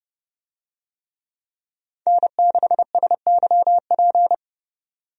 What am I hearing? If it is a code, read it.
Morse code, heard as a keyed tone, about 30 wpm, 700 Hz. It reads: N6SYP
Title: USA Call Signs - 5 Characters - 30wpm